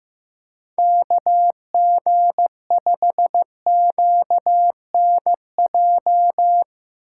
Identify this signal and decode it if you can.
Morse code, heard as a keyed tone, about 15 wpm, 700 Hz. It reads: KG5QNJ
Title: USA Call Signs - 6 Characters - 2x Character Spacing - 15wpm